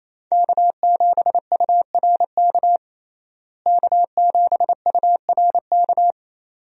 Morse code, heard as a keyed tone, about 28 wpm, 700 Hz. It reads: K7URK K7URK
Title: USA Call Signs - 5 Characters - Repeated 2x - 28wpm